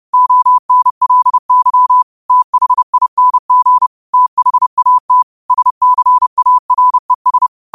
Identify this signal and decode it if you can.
Morse code, heard as a keyed tone, about 30 wpm, 1000 Hz. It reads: ONLY THING THAT SCARES
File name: qotdfiles / 30wpm